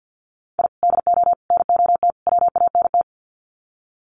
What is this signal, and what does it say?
Morse code, heard as a keyed tone, about 50 wpm, 700 Hz. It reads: I DO NOT WANT
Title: Sets of 4 Words 50wpm